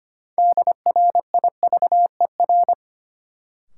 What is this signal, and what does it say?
Morse code, heard as a keyed tone, about 25 wpm, 700 Hz. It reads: DRIVER
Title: Sentences from Top 1000 Words - New Word Drill - 25wpm